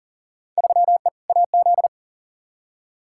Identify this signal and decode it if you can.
Morse code, heard as a keyed tone, about 37 wpm, 700 Hz. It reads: 3E AZ
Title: Field Day Exchanges - 40wpm